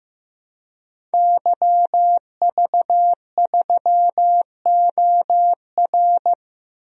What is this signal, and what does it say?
Morse code, heard as a keyed tone, about 15 wpm, 700 Hz. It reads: YV3OR
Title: DXCC-100 Call Signs - 5 Characters - Repeated 3x - 15wpm